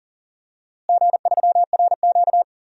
Morse code, heard as a keyed tone, about 40 wpm, 700 Hz. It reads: G3RQ
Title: DXCC-100 Call Signs - 4 Characters - Repeated 3x - 40wpm